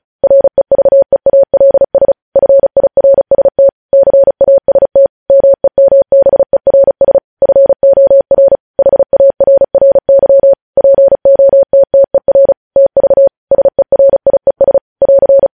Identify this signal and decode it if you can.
Morse code, heard as a keyed tone, about 36 wpm, 550 Hz. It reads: REVEALS FIRST CAST MEMBERS FOR HARRY POTTER TV SERIES <AR>
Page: news headlines in morse code at 35 wpm